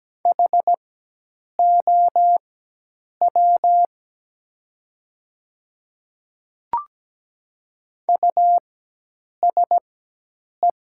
Morse code, heard as a keyed tone, about 17 wpm, spaced as for 8 wpm, 700 Hz. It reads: HOW USE
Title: 4x Character Spacing - Top 100 Words - 17wpm